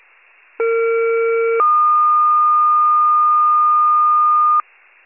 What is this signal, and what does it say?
Two-tone page, heard as a one-second tone, then a three-second tone, about 460 then 1220 Hz. It is heard from 0.6 to 4.6 s.